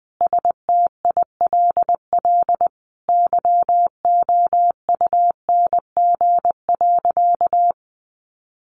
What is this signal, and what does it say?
Morse code, heard as a keyed tone, about 20 wpm, 700 Hz. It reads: STILL YOUNG.
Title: Sentences from Top 400 Words 20wpm